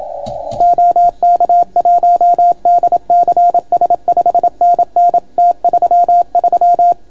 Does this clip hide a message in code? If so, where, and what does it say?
Yes: Morse code from 0.6 to 6.9 s, heard as a keyed tone, about 28 wpm, 700 Hz. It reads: OK1DCS 5NN T33